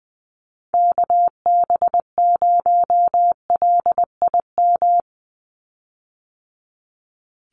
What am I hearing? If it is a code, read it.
Morse code, heard as a keyed tone, about 20 wpm, 700 Hz. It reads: KB0LIM
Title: USA Call Signs 20wpm